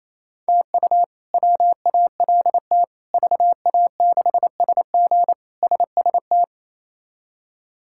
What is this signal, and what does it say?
Morse code, heard as a keyed tone, about 28 wpm, 700 Hz. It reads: TU WALT VA6SG SST
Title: K1USN SST QSO Practice - 28wpm